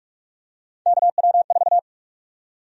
Morse code, heard as a keyed tone, about 41 wpm, 700 Hz. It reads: KW4
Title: USA Call Sign Prefixes - 45wpm